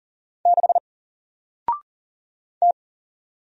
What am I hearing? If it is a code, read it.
Morse code, heard as a keyed tone, about 40 wpm, 700 Hz. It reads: 6 T
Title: Single Character (Letter:Number:Punctuation:Prosign) Alt 40wpm